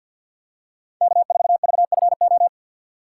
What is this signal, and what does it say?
Morse code, heard as a keyed tone, about 54 wpm, 700 Hz. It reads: K4VFO
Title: USA Call Signs - 5 Characters - 8x Character Spacing - 50wpm